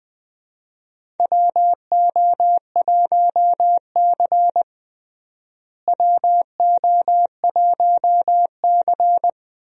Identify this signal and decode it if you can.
Morse code, heard as a keyed tone, about 21 wpm, 700 Hz. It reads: WO1C WO1C
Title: USA Call Signs - 4 Characters - Repeated 3x - 20wpm